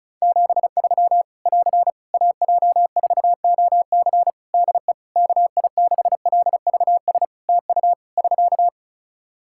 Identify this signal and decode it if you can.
Morse code, heard as a keyed tone, about 34 wpm, 700 Hz. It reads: 73 <AR> AJ4OC DE KI6LVS TU <SK>
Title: Practice QSO - Exchange 4 - Ending - 35wpm